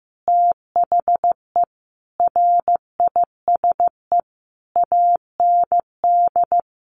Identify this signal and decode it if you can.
Morse code, heard as a keyed tone, about 15 wpm, 700 Hz. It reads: THE RISE AND
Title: Sentences from Top 1500 Words 15wpm